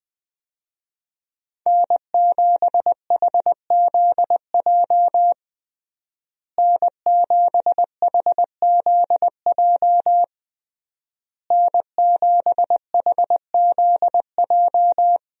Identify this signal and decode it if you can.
Morse code, heard as a keyed tone, about 20 wpm, 700 Hz. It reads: N7HZJ N7HZJ N7HZJ
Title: USA Call Signs - 5 Characters - Repeated 3x - 20wpm